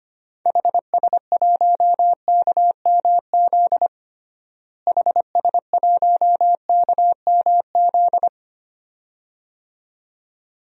Morse code, heard as a keyed tone, about 25 wpm, 700 Hz. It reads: HS1KMZ HS1KMZ
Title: DXCC-100 Call Signs - 6 Characters - Repeated 2x - 25wpm